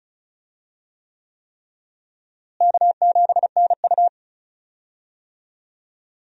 Morse code, heard as a keyed tone, about 36 wpm, 700 Hz. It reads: K7NU
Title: USA Call Signs - 4 Characters - Repeated 3x - 35wpm